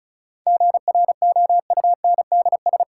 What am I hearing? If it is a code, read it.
Morse code, heard as a keyed tone, about 34 wpm, 700 Hz. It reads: GROUNDS
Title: Sentences from Top 1300 Words - New Word Drill Alt - 35wpm